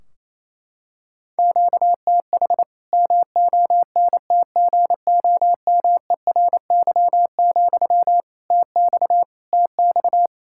Morse code, heard as a keyed tone, about 28 wpm, 700 Hz. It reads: QTH MONTGOMERY, TX TX
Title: Practice QSO - Exchange 1 - Beginning - 28wpm